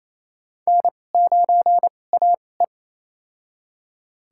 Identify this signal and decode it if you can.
Morse code, heard as a keyed tone, about 28 wpm, spaced as for 20 wpm, 700 Hz. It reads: N9AE
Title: USA Call Signs - 4 Characters - 2x Character Spacing - 28wpm